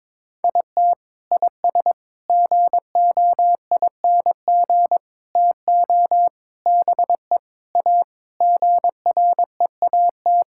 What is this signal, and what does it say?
Morse code, heard as a keyed tone, about 22 wpm, 700 Hz. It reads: IT IS GOING TO BE A GREAT
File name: practice / 22wpm / Sentences with 3-Word Phrases - 22wpm